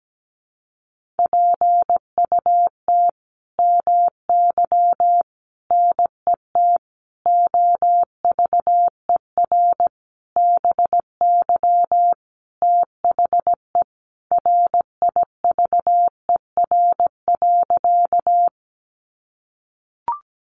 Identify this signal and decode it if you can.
Morse code, heard as a keyed tone, about 17 wpm, 700 Hz. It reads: PUT MY NET OVER BY THE RIVER.
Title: Sentences from Top 1400 Words 17wpm